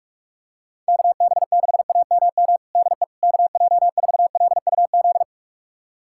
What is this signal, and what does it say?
Morse code, heard as a keyed tone, about 46 wpm, 700 Hz. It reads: KB6AMM DE KJ4LUZ
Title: USA Callsign DE Callsign - 6 Characters - 45wpm